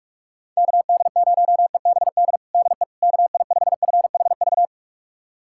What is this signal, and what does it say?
Morse code, heard as a keyed tone, about 41 wpm, 700 Hz. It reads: KD0EBD DE KI5FHV
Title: USA Callsign DE Callsign - 6 Characters - 45wpm